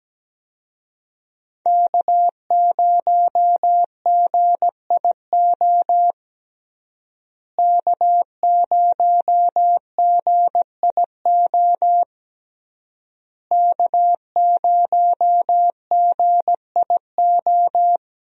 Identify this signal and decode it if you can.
Morse code, heard as a keyed tone, about 17 wpm, 700 Hz. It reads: K0GIO K0GIO K0GIO
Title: USA Call Signs - 5 Characters - Repeated 3x - 17wpm